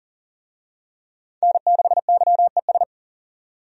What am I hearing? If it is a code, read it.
Morse code, heard as a keyed tone, about 40 wpm, 700 Hz. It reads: N6YES
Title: USA Call Signs - 5 Characters - 40wpm